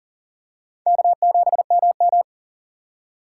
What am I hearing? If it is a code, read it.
Morse code, heard as a keyed tone, about 39 wpm, 700 Hz. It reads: K7MM
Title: USA Call Signs - 4 Characters - 8x Character Spacing - 40wpm